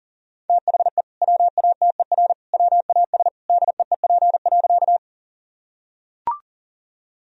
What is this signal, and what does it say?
Morse code, heard as a keyed tone, about 37 wpm, 700 Hz. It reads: THE WATER WAS DEEP.
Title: Sentences from Top 1000 Words - Review Part 3 - 40wpm